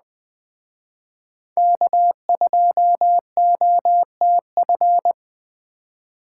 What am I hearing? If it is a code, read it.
Morse code, heard as a keyed tone, about 20 wpm, 700 Hz. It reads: K2OTF